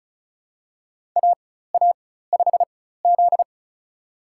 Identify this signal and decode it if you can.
Morse code, heard as a keyed tone, about 36 wpm, spaced as for 16 wpm, 700 Hz. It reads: AA5Z
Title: USA Call Signs - 4 Characters - 4x Character Spacing - 35wpm